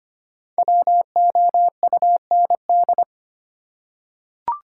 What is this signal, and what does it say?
Morse code, heard as a keyed tone, about 25 wpm, 700 Hz. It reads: WOUND
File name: 5 Letter Words - First 3 Letters Given - 25wpm